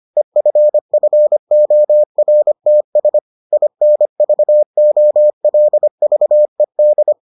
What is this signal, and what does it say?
Morse code, heard as a keyed tone, about 25 wpm, 600 Hz. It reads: EFFORTS INVOLVED